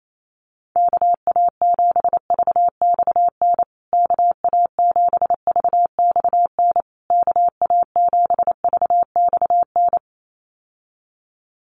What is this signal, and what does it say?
Morse code, heard as a keyed tone, about 28 wpm, 700 Hz. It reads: KA7VXN KA7VXN KA7VXN
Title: USA Call Signs Repeated 3x 28wpm